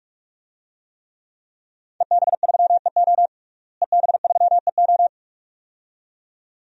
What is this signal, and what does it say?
Morse code, heard as a keyed tone, about 46 wpm, 700 Hz. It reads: EB3EO EB3EO
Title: DXCC-100 Call Signs - 5 Characters - Repeated 2x - 45wpm